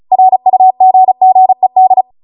Morse code, heard as a keyed tone, about 36 wpm, 750 Hz. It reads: RUGGED